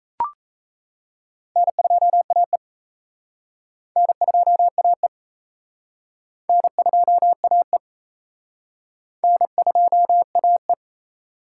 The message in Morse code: N2AE N2AE N2AE N2AE

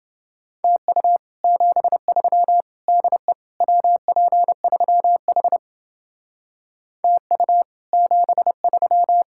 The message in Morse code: TU 73 DE WP3H TU 73

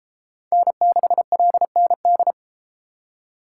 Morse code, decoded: N6LND

